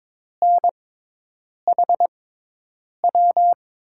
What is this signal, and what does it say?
Morse code, heard as a keyed tone, about 22 wpm, spaced as for 8 wpm, 700 Hz. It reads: NHW
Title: USA Call Signs - 6 Characters - 6x Character Spacing - 22wpm